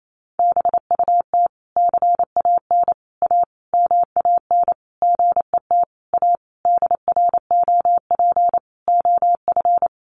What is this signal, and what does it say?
Morse code, heard as a keyed tone, about 28 wpm, 700 Hz. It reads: BUT CAN A MAN GET A DROP OF
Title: Sentences from Top 1400 Words 28wpm